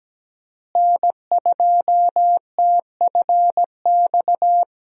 Morse code, heard as a keyed tone, about 17 wpm, 700 Hz. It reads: N2TFX
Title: USA Call Signs - 5 Characters - Repeated 2x - 17wpm